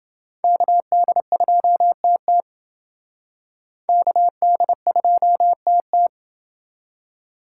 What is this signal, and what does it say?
Morse code, heard as a keyed tone, about 29 wpm, 700 Hz. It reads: KD2TT KD2TT